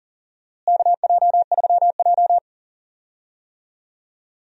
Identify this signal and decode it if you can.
Morse code, heard as a keyed tone, about 39 wpm, 700 Hz. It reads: KJ3J
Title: USA Call Signs - 4 Characters - Repeated 3x - 40wpm